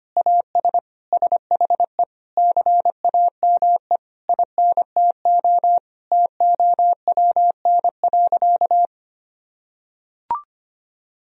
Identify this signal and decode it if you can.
Morse code, heard as a keyed tone, about 25 wpm, 700 Hz. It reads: AS SHE CAME INTO TOWN.
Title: Sentences from Top 1000 Words - Review Part 2 - 25wpm